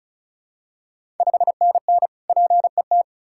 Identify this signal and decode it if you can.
Morse code, heard as a keyed tone, about 36 wpm, 700 Hz. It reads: 5NN PET
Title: Ontario QSO Party Exchange 35wpm